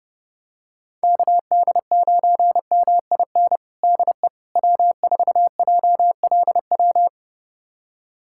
Morse code, heard as a keyed tone, about 30 wpm, 700 Hz. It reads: KD9MIN DE W4JLW
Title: USA Callsign DE Callsign 30wpm